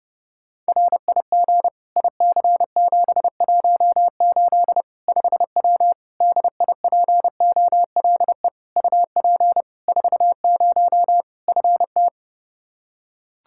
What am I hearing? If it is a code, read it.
Morse code, heard as a keyed tone, about 30 wpm, 700 Hz. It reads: RIG IC718 5W DIPOLE UP 40 FT
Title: K1USN SST QSO Practice 30wpm